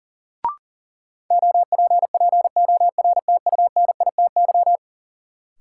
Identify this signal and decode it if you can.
Morse code, heard as a keyed tone, about 40 wpm, 700 Hz. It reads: OPPORTUNITY
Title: Sentences from Top 1000 Words - New Word Drill - 40wpm